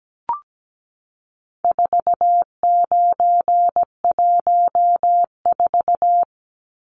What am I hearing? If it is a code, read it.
Morse code, heard as a keyed tone, about 17 wpm, 700 Hz. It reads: 4914
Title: Numbers - 4-Digit 17wpm